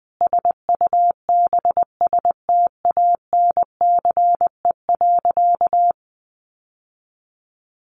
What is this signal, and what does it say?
Morse code, heard as a keyed tone, about 20 wpm, 700 Hz. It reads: SUBSTANCE.
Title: Sentences from Top 1500 Words 20wpm